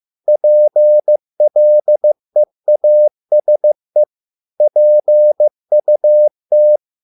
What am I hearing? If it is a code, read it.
Morse code, heard as a keyed tone, about 15 wpm, 600 Hz. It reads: PLEASE PUT